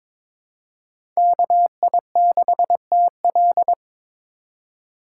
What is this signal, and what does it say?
Morse code, heard as a keyed tone, about 22 wpm, 700 Hz. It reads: KI6TL
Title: USA Call Signs - 4x Character Spacing - 22wpm